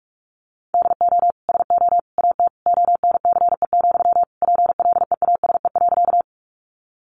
Morse code, heard as a keyed tone, about 44 wpm, 700 Hz. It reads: DO SO AT ONCE, PLEASE.
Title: Sentences from Top 1500 Words 45wpm